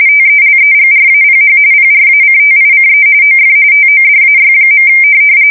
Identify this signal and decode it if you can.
RTTY, about 45.45 baud. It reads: TELEGRAPH PRACTICE OF TODAY.
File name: fsk170 / printing-telegraph